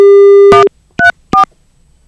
DTMF, heard as telephone touch tones, tones of about 120 ms. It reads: 134